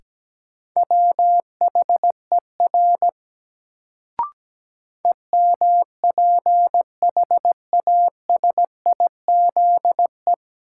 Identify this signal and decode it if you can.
Morse code, heard as a keyed tone, about 17 wpm, 700 Hz. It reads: WHER EMPHASIZE
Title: Sentences from Top 1500 Words - New Word Drill - 17wpm